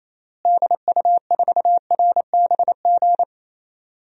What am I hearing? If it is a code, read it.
Morse code, heard as a keyed tone, about 28 wpm, 700 Hz. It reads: DU4RBG
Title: DXCC-100 Call Signs - 6 Characters - Repeated 3x - 28wpm